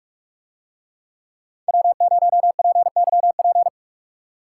Morse code, heard as a keyed tone, about 42 wpm, 700 Hz. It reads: W0PYP